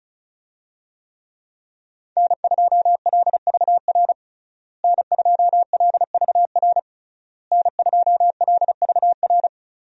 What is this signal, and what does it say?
Morse code, heard as a keyed tone, about 34 wpm, 700 Hz. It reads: N2LVR N2LVR N2LVR
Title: USA Call Signs - 5 Characters - Repeated 3x - 35wpm